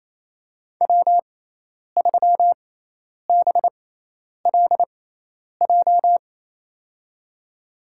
Morse code, heard as a keyed tone, about 28 wpm, spaced as for 10 wpm, 700 Hz. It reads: W3BLJ
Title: USA Call Signs - 6x Character Spacing - 28wpm